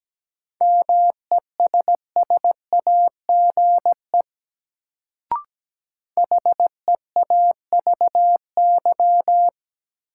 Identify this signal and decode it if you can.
Morse code, heard as a keyed tone, about 17 wpm, 700 Hz. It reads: MESSAGE HEAVY
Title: Sentences from Top 1100 Words - New Word Drill Alt - 17wpm